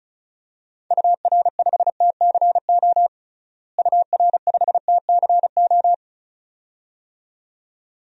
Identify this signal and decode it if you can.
Morse code, heard as a keyed tone, about 34 wpm, 700 Hz. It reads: UR5TCO UR5TCO
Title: DXCC-100 Call Signs - 6 Characters - Repeated 2x - 35wpm